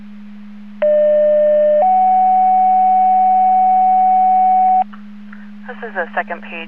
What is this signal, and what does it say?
Two-tone page, heard as a one-second tone, then a three-second tone, about 600 then 750 Hz, with speech. It is heard from 0.8 to 4.8 s.